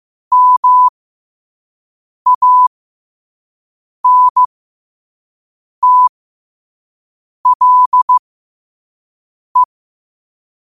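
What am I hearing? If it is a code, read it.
Morse code, heard as a keyed tone, about 15 wpm, spaced as for 5 wpm, 1000 Hz. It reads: MANTLE
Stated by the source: Morse code practice at 5 wpm.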